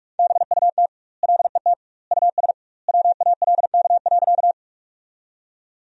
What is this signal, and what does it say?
Morse code, heard as a keyed tone, about 43 wpm, 700 Hz. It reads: BUT LET US WALK.
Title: 2x Word-Spacing - Sentences from Top 300 Words - 45wpm